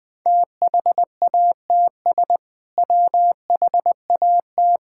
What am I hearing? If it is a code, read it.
Morse code, heard as a keyed tone, about 20 wpm, 700 Hz. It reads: THATS WHAT
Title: Sentences with 2-Word Phrases - 20wpm